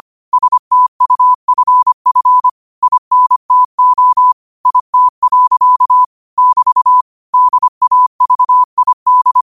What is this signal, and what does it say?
Morse code, heard as a keyed tone, about 25 wpm, 1000 Hz. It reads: STUFF INTO IT. <BT> DAVID